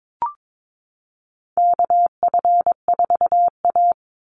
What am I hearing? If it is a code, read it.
Morse code, heard as a keyed tone, about 22 wpm, 700 Hz. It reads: KF4A
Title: Call Signs Repeated 2x - 22wpm